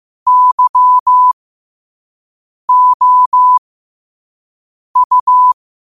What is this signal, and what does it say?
Morse code, heard as a keyed tone, about 15 wpm, spaced as for 5 wpm, 1000 Hz. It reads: YOU
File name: qotdfiles / 5wpm